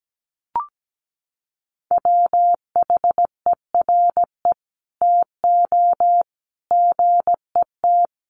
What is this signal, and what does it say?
Morse code, heard as a keyed tone, about 17 wpm, 700 Hz. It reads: WHERE TO GET
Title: Sentences from Top 1400 Words 17wpm